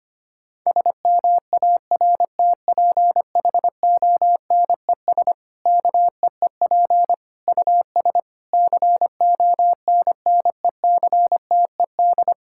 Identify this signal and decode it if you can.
Morse code, heard as a keyed tone, about 25 wpm, 700 Hz. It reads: SMARTPHONES KEEP US CONNECTED
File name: practice / 25wpm / Difficult Sentences with Hints - 25wpm